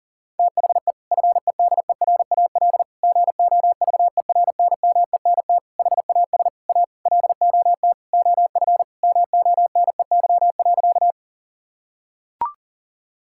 Morse code, heard as a keyed tone, about 40 wpm, 700 Hz. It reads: THE FEDERAL GOVERNMENT HAS A LOT OF MONEY.
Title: Sentences from Top 1000 Words - Review Part 5 - 40wpm